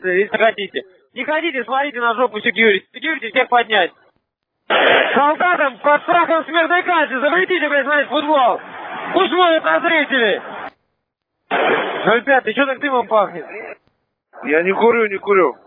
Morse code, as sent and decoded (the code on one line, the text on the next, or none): none